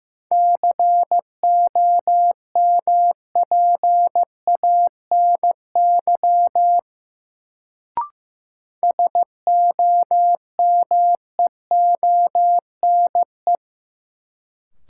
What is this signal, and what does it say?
Morse code, heard as a keyed tone, about 15 wpm, 700 Hz. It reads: COMPANY SOMEONE
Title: Sentences from Top 600 Words - New Word Drill - 15wpm